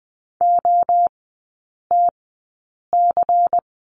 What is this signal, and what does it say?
Morse code, heard as a keyed tone, about 20 wpm, spaced as for 8 wpm, 700 Hz. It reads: OTC
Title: 191 Lesson 31e - Letter v - Three Characters 20wpm